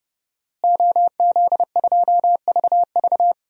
Morse code, heard as a keyed tone, about 30 wpm, 700 Hz. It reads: OZ2VV
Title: DXCC-100 Call Signs - 5 Characters - Repeated 2x - 30wpm